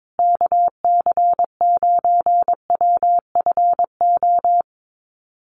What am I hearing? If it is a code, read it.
Morse code, heard as a keyed tone, about 22 wpm, 700 Hz. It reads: KC9WFO
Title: USA Call Signs Repeated 3x 22wpm